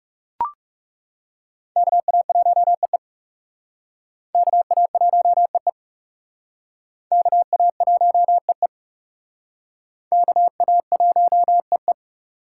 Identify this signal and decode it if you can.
Morse code, heard as a keyed tone, about 36 wpm, 700 Hz. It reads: KA1EE KA1EE KA1EE KA1EE
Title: Speed Racing - USA 2x2 Call Signs 30wpm